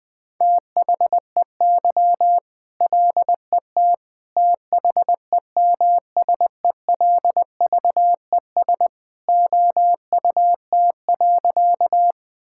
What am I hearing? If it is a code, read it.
Morse code, heard as a keyed tone, about 20 wpm, 700 Hz. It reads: THEY LET THEMSELVES OUT.
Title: Sentences from Top 1000 Words - Review Part 5 - 20wpm